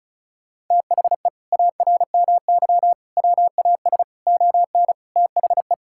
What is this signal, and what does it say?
Morse code, heard as a keyed tone, about 34 wpm, 700 Hz. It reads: THE ARMY WAS ON THE
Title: Sentences from Top 1000 Words - Review Part 5 - 35wpm